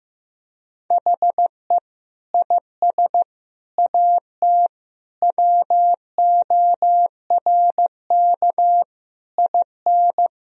Morse code, heard as a keyed tone, about 16 wpm, 700 Hz. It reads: HE IS AT WORK IN
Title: Sentences from Top 1000 Words - Review Part 4 - 15wpm